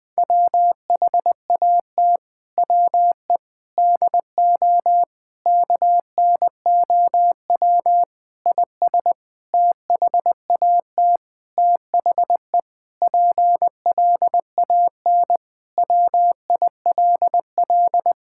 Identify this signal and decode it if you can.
Morse code, heard as a keyed tone, about 20 wpm, 700 Hz. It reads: WHAT WE DO KNOW IS THAT THE PLAN WILL
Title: Sentences with 7-Word Phrases - Spoken Beginning - 20wpm